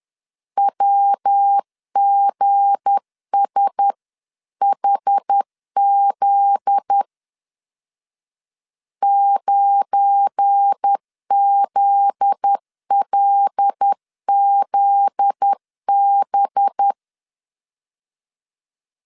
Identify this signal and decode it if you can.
Morse code, heard as a keyed tone, about 10 wpm, 800 Hz. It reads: WGSHZ 9ZLZB
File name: LEZIONI / 7 GF9Z / 7 GF9Z 10wpm 2611